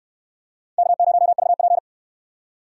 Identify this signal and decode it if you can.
Morse code, heard as a keyed tone, about 66 wpm, 700 Hz. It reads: K0VP